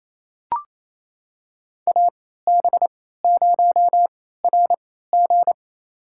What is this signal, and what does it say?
Morse code, heard as a keyed tone, about 28 wpm, spaced as for 16 wpm, 700 Hz. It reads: AB0RG
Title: USA Call Signs - 3x Character Spacing - 28wpm